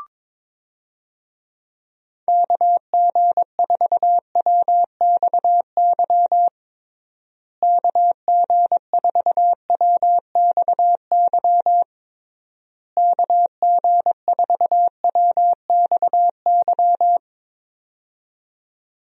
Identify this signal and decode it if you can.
Morse code, heard as a keyed tone, about 21 wpm, 700 Hz. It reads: KG4WXY KG4WXY KG4WXY
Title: USA Call Signs - 6 Characters - Repeated 3x - 22wpm